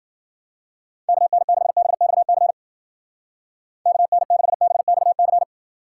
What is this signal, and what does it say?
Morse code, heard as a keyed tone, about 50 wpm, 700 Hz. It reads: KN6BXC KN6BXC
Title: Speed Racing - USA 2x3 Call Signs 40wpm